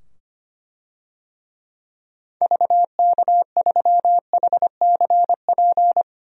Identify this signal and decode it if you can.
Morse code, heard as a keyed tone, about 25 wpm, 700 Hz. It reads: VK3HCP
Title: DXCC-100 Call Signs - 6 Characters - 25wpm